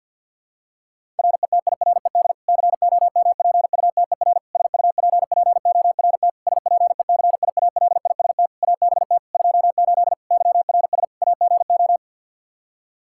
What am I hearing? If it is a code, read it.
Morse code, heard as a keyed tone, about 51 wpm, 700 Hz. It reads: RETIRED COMPUTER SUPPORT SPECIALIST ABT 28 YRS AGO
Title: Practice QSO - Exchange 3 - More Info - 50wpm